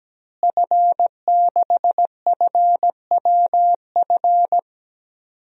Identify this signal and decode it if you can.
Morse code, heard as a keyed tone, about 17 wpm, 700 Hz. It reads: F6FWF
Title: DXCC-100 Call Signs - 5 Characters - Repeated 2x - 17wpm